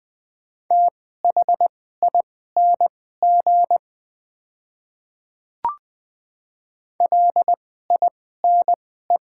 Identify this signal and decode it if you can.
Morse code, heard as a keyed tone, about 20 wpm, spaced as for 15 wpm, 700 Hz. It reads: THING LINE
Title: 2x Character Spacing - Top 200 Words - 20wpm